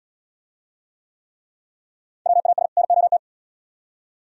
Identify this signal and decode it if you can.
Morse code, heard as a keyed tone, about 76 wpm, 700 Hz. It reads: WAS NOT